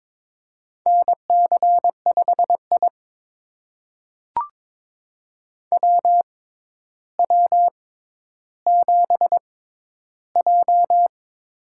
Morse code, heard as a keyed tone, about 22 wpm, spaced as for 11 wpm, 700 Hz. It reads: NC5I WW7J
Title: USA Call Signs - 4 Characters - 6x Character Spacing - 22wpm